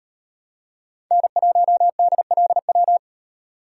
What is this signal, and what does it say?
Morse code, heard as a keyed tone, about 35 wpm, 700 Hz. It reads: N1DLW